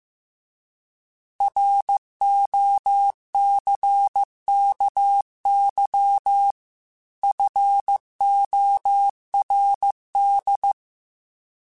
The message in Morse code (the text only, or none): ROCKY FORD